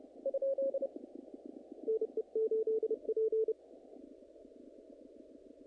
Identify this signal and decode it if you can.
Morse code, heard as a keyed tone, about 31 wpm, 440 Hz. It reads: NE8P
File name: contests / naqp-cw-k5tr-op-k5pi- -1900